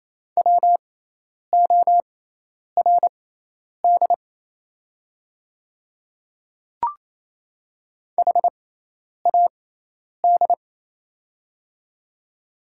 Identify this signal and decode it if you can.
Morse code, heard as a keyed tone, about 27 wpm, spaced as for 10 wpm, 700 Hz. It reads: WORD HAD
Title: 6x Character Spacing - Top 100 Words - 28wpm